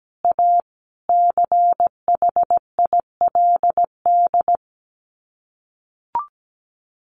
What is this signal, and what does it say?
Morse code, heard as a keyed tone, about 17 wpm, 690 Hz. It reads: A CHILD
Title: Sets of 5 Words Encore 17wpm